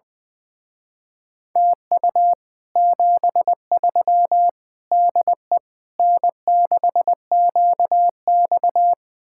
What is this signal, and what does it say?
Morse code, heard as a keyed tone, about 20 wpm, 700 Hz. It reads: TU 73 DE N6QX
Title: Speed-Racing Common QSO Phrases - 20wpm